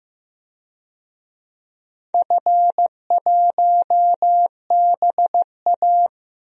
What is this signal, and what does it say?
Morse code, heard as a keyed tone, about 15 wpm, 700 Hz. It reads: F1BA
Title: DXCC-100 Call Signs - 4 Characters - Repeated 3x - 15wpm